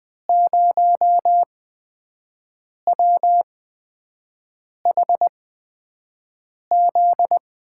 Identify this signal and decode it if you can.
Morse code, heard as a keyed tone, about 20 wpm, spaced as for 5 wpm, 700 Hz. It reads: 0WHZ
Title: USA Call Signs - 5 Characters - 8x Character Spacing - 20wpm